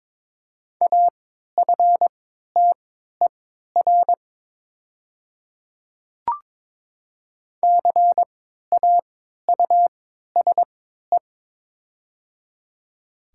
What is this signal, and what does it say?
Morse code, heard as a keyed tone, about 22 wpm, spaced as for 13 wpm, 700 Hz. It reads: AFTER CAUSE
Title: 3x Character Spacing - Top 200 Words - 22wpm